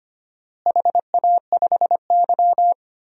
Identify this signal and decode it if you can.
Morse code, heard as a keyed tone, about 25 wpm, 700 Hz. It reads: HA5Y
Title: DXCC-100 Call Signs - 4 Characters - Repeated 3x - 25wpm